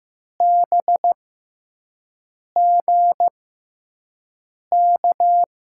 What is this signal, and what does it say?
Morse code, heard as a keyed tone, about 16 wpm, spaced as for 5 wpm, 700 Hz. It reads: BGK
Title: USA Call Signs - 6 Characters - 6x Character Spacing - 15wpm